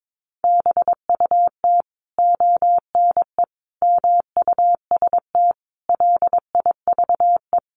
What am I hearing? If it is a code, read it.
Morse code, heard as a keyed tone, about 22 wpm, 700 Hz. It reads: BUT ONE MUST LIVE
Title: Sentences from Top 1400 Words 22wpm